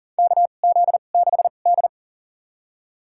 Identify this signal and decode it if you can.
Morse code, heard as a keyed tone, about 40 wpm, spaced as for 29 wpm, 700 Hz. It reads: KZ6D
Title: USA Call Signs - 4 Characters - 2x Character Spacing - 40wpm